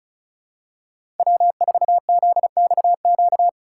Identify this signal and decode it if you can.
Morse code, heard as a keyed tone, about 36 wpm, 700 Hz. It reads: W4ZXQ
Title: USA Call Signs - 5 Characters - 4x Character Spacing - 35wpm